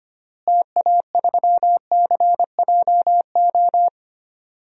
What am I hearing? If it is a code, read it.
Morse code, heard as a keyed tone, about 25 wpm, 700 Hz. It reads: TA3CJO
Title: DXCC-100 Call Signs - 6 Characters - Repeated 3x - 25wpm